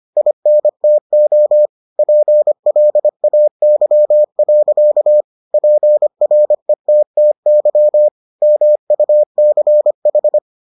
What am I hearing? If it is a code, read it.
Morse code, heard as a keyed tone, about 25 wpm, 600 Hz. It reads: INTO PLAY. PRETTY MUCH